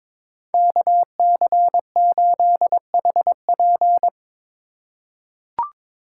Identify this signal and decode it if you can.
Morse code, heard as a keyed tone, about 22 wpm, 700 Hz. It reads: KC8HP
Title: USA Call Signs - 5 Characters - 2x Character Spacing - 22wpm